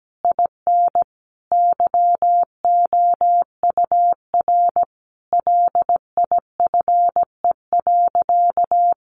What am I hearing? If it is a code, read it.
Morse code, heard as a keyed tone, about 17 wpm, 700 Hz. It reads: IN YOUR LIFE.
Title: Sentences from Top 700 Words 17wpm